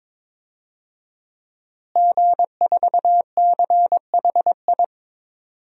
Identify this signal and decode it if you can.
Morse code, heard as a keyed tone, about 21 wpm, 700 Hz. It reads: G4CHI
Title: DXCC-100 Call Signs - 5 Characters - Repeated 2x - 22wpm